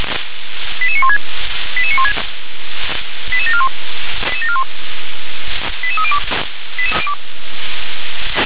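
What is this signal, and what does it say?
Five-tone selective call (ZVEI), heard as quick runs of five tones, tones of about 70 ms. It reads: 88916 88916 88741 88302
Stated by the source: ZVEI five-tone selective call 88926, received by SDR.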